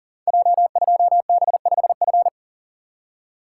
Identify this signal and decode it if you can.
Morse code, heard as a keyed tone, about 38 wpm, 690 Hz. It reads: J2B5F
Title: Code Groups - 5 Letter:Numbers - 40wpm